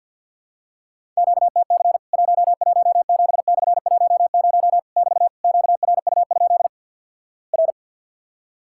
Morse code, heard as a keyed tone, about 54 wpm, 700 Hz. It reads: QTC 117/10 <BT> QRU? R
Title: Worked All Europe DX Contest - QTC Exchange - 50wpm